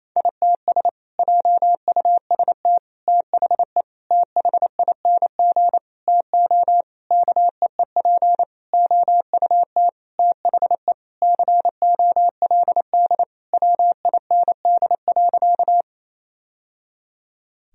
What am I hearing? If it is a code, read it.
Morse code, heard as a keyed tone, about 28 wpm, 700 Hz. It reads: ITS JUST THE THING TO KEEP OUT THE COLD WIND.
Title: Sentences from Top 1000 Words - Review Part 2 - 28wpm